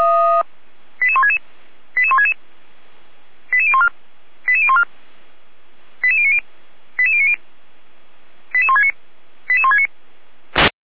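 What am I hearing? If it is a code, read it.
Five-tone selective call (ZVEI), heard as quick runs of five tones, tones of about 70 ms. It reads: 88166 88166 88014 88014 88090 88090 88169 88169
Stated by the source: ZVEI five-tone call received by SDR, from the audio server of feuerwehr-orlamuende.org (fire brigade Orlamünde).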